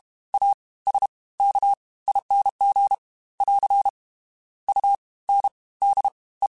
Morse code, heard as a keyed tone, about 32 wpm, 800 Hz. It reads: A S K ING <AR> U N D E
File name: Special INGs / Special INGs-mp3s / Fast ing @30